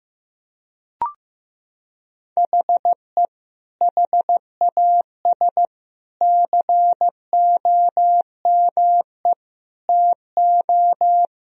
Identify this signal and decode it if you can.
Morse code, heard as a keyed tone, about 16 wpm, 700 Hz. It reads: HE HAS COME TO
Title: Sentences from Top 1000 Words - Review Part 1 - 15wpm